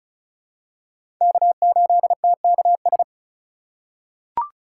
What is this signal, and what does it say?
Morse code, heard as a keyed tone, about 36 wpm, 700 Hz. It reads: K8TKS
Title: USA Call Signs - 5 Characters - 4x Character Spacing - 35wpm